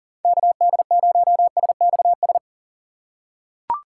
Morse code, heard as a keyed tone, about 40 wpm, 700 Hz. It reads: KD0SXS
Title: USA Call Signs - 6 Characters - 4x Character Spacing - 40wpm